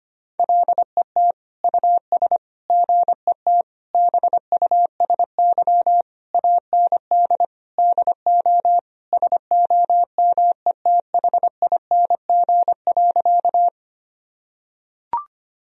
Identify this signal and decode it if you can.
Morse code, heard as a keyed tone, about 25 wpm, 700 Hz. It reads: LET US GET BUSY AND DO SOMETHING.
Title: Sentences from Top 1000 Words - Review Part 3 - 25wpm